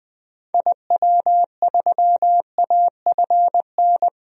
Morse code, heard as a keyed tone, about 20 wpm, 700 Hz. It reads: IW3AFN
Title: DXCC-100 Call Signs - 6 Characters - 20wpm